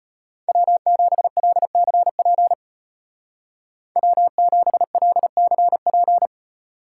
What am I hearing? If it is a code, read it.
Morse code, heard as a keyed tone, about 33 wpm, 700 Hz. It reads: W7LCP W7LCP